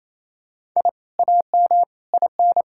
Morse code, heard as a keyed tone, about 28 wpm, 700 Hz. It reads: I AM IN
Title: Sentences from Top 1000 Words - Review Part 6 - 28wpm